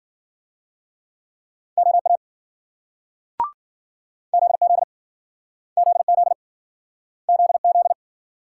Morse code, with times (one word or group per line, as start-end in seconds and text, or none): 1.8-2.2 OA
4.3-4.8 ZZ
5.8-6.3 ZZ
7.3-7.9 ZZ